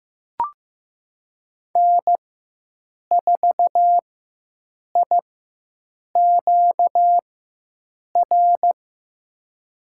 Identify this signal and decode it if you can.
Morse code, heard as a keyed tone, about 15 wpm, spaced as for 7 wpm, 700 Hz. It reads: N4IQR
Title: USA Call Signs - 4x Character Spacing - 15wpm